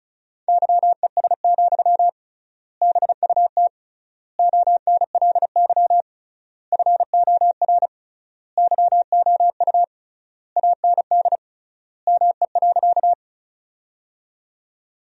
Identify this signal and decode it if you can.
Morse code, heard as a keyed tone, about 36 wpm, 700 Hz. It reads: YES, BUT ONLY FOR YOU AND ME.
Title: 3x Word-Spacing - Sentences from Top 500 Words - 35wpm